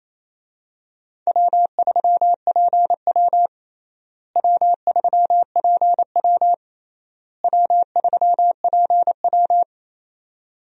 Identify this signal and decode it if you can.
Morse code, heard as a keyed tone, about 28 wpm, 700 Hz. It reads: W3PW W3PW W3PW
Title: USA Call Signs - 4 Characters - Repeated 3x - 28wpm